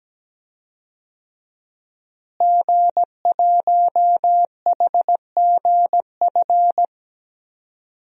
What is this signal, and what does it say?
Morse code, heard as a keyed tone, about 17 wpm, 700 Hz. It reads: G1HGF
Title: DXCC-100 Call Signs - 5 Characters - Repeated 2x - 17wpm